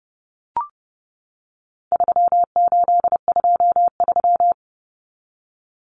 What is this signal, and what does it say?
Morse code, heard as a keyed tone, about 30 wpm, 700 Hz. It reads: 3823
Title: Common Cut Numbers - 4-Digit 30wpm